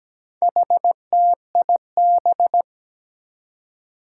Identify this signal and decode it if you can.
Morse code, heard as a keyed tone, about 17 wpm, 700 Hz. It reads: HTIB
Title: Four Challenging Letter-Number Combinations 17wpm